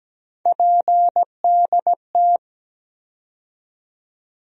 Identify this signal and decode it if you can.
Morse code, heard as a keyed tone, about 17 wpm, 700 Hz. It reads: PDT
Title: USA Call Sign Postfixes - 3 Letters 17wpm